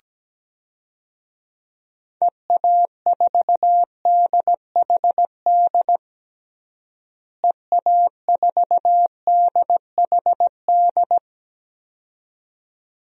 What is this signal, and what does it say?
Morse code, heard as a keyed tone, about 17 wpm, 700 Hz. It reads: EA4DHD EA4DHD
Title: DXCC-100 Call Signs - 6 Characters - Repeated 2x - 17wpm